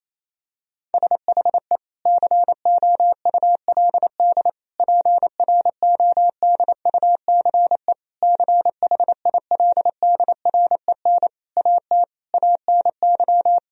Morse code, heard as a keyed tone, about 28 wpm, 700 Hz. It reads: SHE COULD PRODUCE CHILDREN AT ANY
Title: Sentences from Top 1000 Words - Review Part 3 - 28wpm